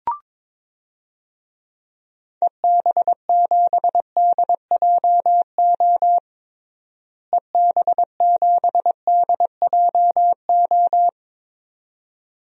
Morse code, heard as a keyed tone, about 22 wpm, 700 Hz. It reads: EB7DJO EB7DJO
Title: DXCC-100 Call Signs - 6 Characters - Repeated 2x - 22wpm